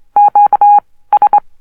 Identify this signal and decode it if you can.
Morse code, heard as a keyed tone, about 23 wpm, spaced as for 16 wpm, 820 Hz. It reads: QS
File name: morse code oscillator low pitched